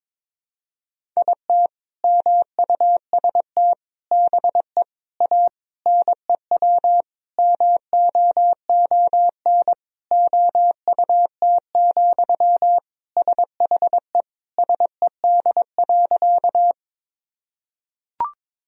Morse code, heard as a keyed tone, about 22 wpm, 700 Hz. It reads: IT MUST BE A NEW MOON OUT, SHE SED.
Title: Sentences from Top 500 Words and Abbreviations 22wpm